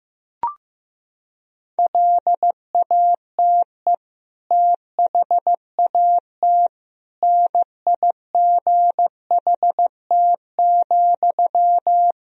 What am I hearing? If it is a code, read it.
Morse code, heard as a keyed tone, about 15 wpm, 700 Hz. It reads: LATE THAT NIGHT,
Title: Sentences from Top 1000 Words - Review Part 4 - 15wpm